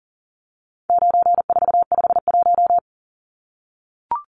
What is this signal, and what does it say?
Morse code, heard as a keyed tone, about 40 wpm, 700 Hz. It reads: 9451
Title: Numbers - 4-Digit 40wpm